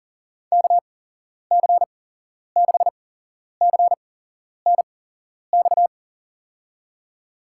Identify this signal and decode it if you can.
Morse code, heard as a keyed tone, about 40 wpm, spaced as for 11 wpm, 700 Hz. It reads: KC6CNX
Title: USA Call Signs - 6 Characters - 8x Character Spacing - 40wpm